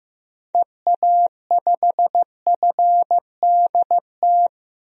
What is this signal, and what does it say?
Morse code, heard as a keyed tone, about 15 wpm, 700 Hz. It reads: EA5FDT
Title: DXCC-100 Call Signs - 6 Characters - Repeated 2x - 15wpm